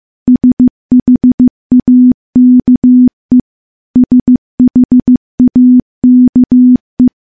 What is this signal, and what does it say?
Morse code, heard as a keyed tone, about 15 wpm, 250 Hz. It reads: SHAKE SHAKE
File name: morse / morse-training-15WPM